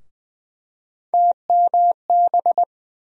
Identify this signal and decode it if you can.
Morse code, heard as a keyed tone, about 20 wpm, 700 Hz. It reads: TMB
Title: USA Call Sign Postfixes - 3 Letters 20wpm